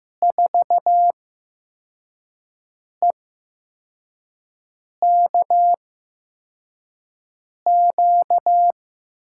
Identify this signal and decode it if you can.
Morse code, heard as a keyed tone, about 15 wpm, spaced as for 4 wpm, 700 Hz. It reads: 4EKQ
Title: USA Call Signs - 6 Characters - 8x Character Spacing - 15wpm